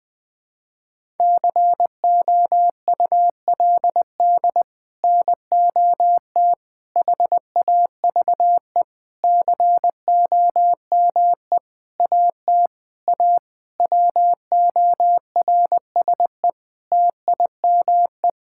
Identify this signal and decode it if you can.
Morse code, heard as a keyed tone, about 20 wpm, 700 Hz. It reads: COULD NOT HAVE COME AT A WORSE TIME
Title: Sets of 8 Words - 20wpm